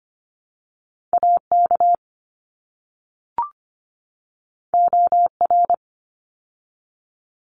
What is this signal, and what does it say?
Morse code, heard as a keyed tone, about 25 wpm, 700 Hz. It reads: AK OR
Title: ARRL Sections 25wpm